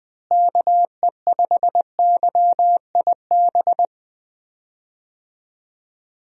Morse code, heard as a keyed tone, about 21 wpm, 700 Hz. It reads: KE5YIB
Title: USA Call Signs - 6 Characters - Repeated 3x - 20wpm